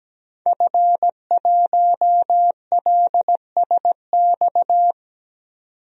Morse code, heard as a keyed tone, about 17 wpm, 700 Hz. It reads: F1LSX
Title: DXCC-100 Call Signs - 5 Characters - 17wpm